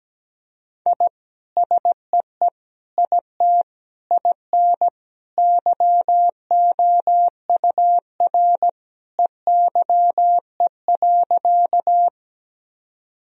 Morse code, heard as a keyed tone, about 17 wpm, 700 Hz. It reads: I SEE IT IN YOUR EYE.
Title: Sentences from Top 1000 Words - Review Part 1 - 17wpm